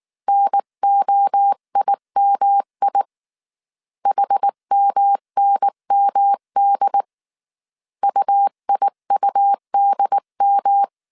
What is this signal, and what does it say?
Morse code, heard as a keyed tone, about 19 wpm, 800 Hz. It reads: NOIMI HMNMD UIUDM
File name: CW / DET / 1 3 1500C S B 18wpm 1916